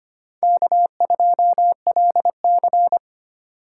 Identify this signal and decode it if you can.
Morse code, heard as a keyed tone, about 25 wpm, 700 Hz. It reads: K2LC
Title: USA Call Signs - 4 Characters - 2x Character Spacing - 25wpm